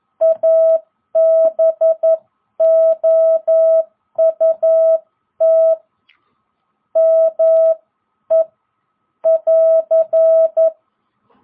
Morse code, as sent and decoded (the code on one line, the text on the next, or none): .- -... --- ..- - / -- . / .-.-.
ABOUT ME <AR>